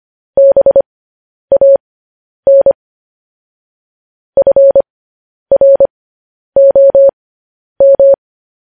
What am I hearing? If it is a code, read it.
Morse code, heard as a keyed tone, about 25 wpm, spaced as for 10 wpm, 550 Hz. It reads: BAN FROM